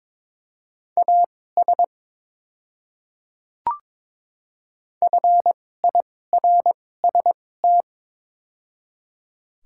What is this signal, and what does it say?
Morse code, heard as a keyed tone, about 22 wpm, spaced as for 16 wpm, 700 Hz. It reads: AS FIRST
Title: 2x Character Spacing - Top 100 Words - 22wpm